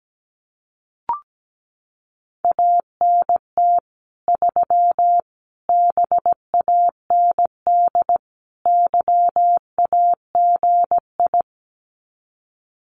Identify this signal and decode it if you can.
Morse code, heard as a keyed tone, about 17 wpm, 700 Hz. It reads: ANT 3 BAND YAGI
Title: Common QSO Phrases 17wpm